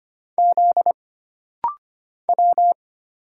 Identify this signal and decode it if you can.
Morse code, heard as a keyed tone, about 25 wpm, 700 Hz. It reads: Z W
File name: Single Character (Letter:Number:Punctuation:Prosign) - Rapid-Fire - 25wpm